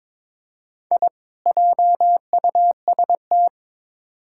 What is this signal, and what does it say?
Morse code, heard as a keyed tone, about 22 wpm, 700 Hz. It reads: I JUST
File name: Build A Phrase - 2-Word Phrases - 22wpm